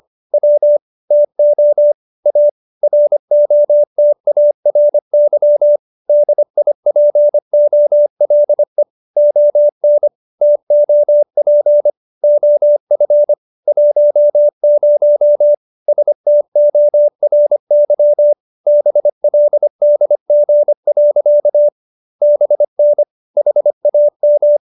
Morse code, heard as a keyed tone, about 25 wpm, 590 Hz. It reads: W TO A ROTARY DIPOLE ON TOP OF 10 STORY BLDG. BN HAM
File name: QSO315 25